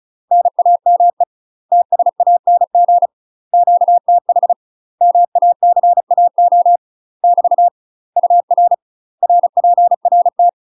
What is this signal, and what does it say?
Morse code, heard as a keyed tone, about 36 wpm, 700 Hz. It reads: NAME TSANG QTH MACAO <BT> UR RPRT
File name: UK MASTER QSOs TRAINING / 35WPM / UK MASTER QSOs 0016